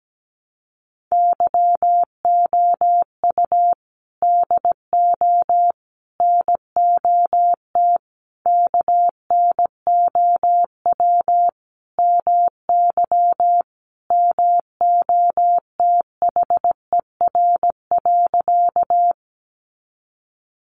Morse code, heard as a keyed tone, about 17 wpm, 700 Hz. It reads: YOU DO NOT KNOW MY MOTHER.
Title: Sentences from Top 600 Words 17wpm